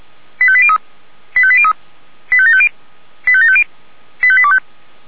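Five-tone selective call (ZVEI), heard as quick runs of five tones, tones of about 70 ms. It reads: 85803 85803 85750 85750 85825
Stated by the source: ZVEI five-tone selective call 85827 received on an SDR.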